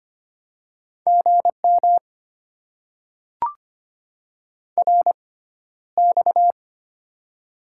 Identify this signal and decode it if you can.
Morse code, heard as a keyed tone, about 25 wpm, spaced as for 13 wpm, 690 Hz. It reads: GM RX
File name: Top 90 QSO Elements - 6x Character Spacing - 25wpm